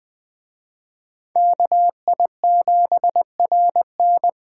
Morse code, heard as a keyed tone, about 20 wpm, 700 Hz. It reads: KI7RN
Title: Speed Racing - USA Call Signs 17wpm